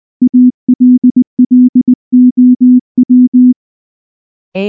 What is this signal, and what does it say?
Morse code, heard as a keyed tone, about 18 wpm, 250 Hz. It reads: ALLOW
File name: morse / morse-training-20WPM-5